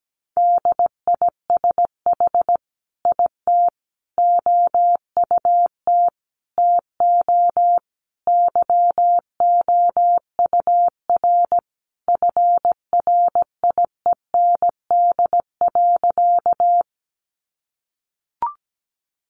Morse code, heard as a keyed tone, about 17 wpm, 700 Hz. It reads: DISH IT OUT TO YOUR FRIEND.
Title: Sentences from Top 1300 Words 17wpm